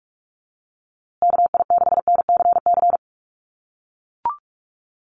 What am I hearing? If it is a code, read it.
Morse code, heard as a keyed tone, about 44 wpm, 700 Hz. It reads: KI6NCC
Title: USA Call Signs Repeated 3x 45wpm